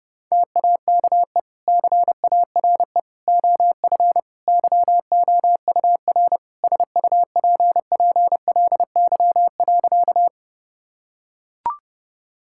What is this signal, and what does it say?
Morse code, heard as a keyed tone, about 30 wpm, 700 Hz. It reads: TAKE CARE OF YOUR SUPPLY.
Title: Sentences from Top 1000 Words - Review Part 5 - 30wpm